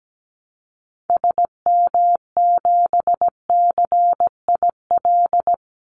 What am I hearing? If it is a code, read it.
Morse code, heard as a keyed tone, about 17 wpm, 700 Hz. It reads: SM7CIL
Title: Call Signs Repeated 2x - 17wpm